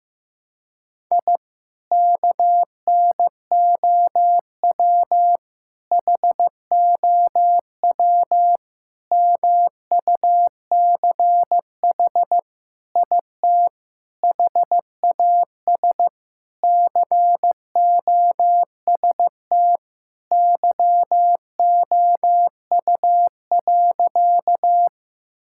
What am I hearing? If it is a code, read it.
Morse code, heard as a keyed tone, about 15 wpm, 700 Hz. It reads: I KNOW HOW MUCH IT HAS COST YOU.
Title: Sentences from Top 1000 Words - Review Part 5 - 15wpm